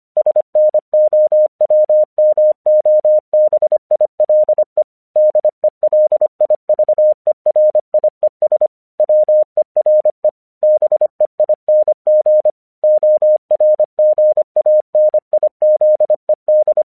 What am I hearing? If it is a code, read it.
Morse code, heard as a keyed tone, about 25 wpm, 610 Hz. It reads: SNOWMOBILE DELIVERIES WERE BEING ORGANIZED
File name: 1 English 25wpm / AQC05